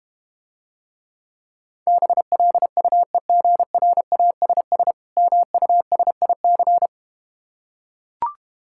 Morse code, heard as a keyed tone, about 31 wpm, 700 Hz. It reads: BLUEGRASS MUSIC